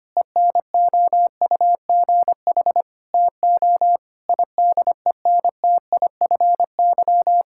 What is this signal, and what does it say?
Morse code, heard as a keyed tone, about 25 wpm, 700 Hz. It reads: ENOUGH TO IDENTIFY
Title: Sentences from Top 1000 Words - Review Part 6 - 25wpm